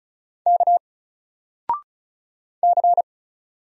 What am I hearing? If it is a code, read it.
Morse code, heard as a keyed tone, about 33 wpm, 700 Hz. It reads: K C
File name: Single-Letter-Number-35wpm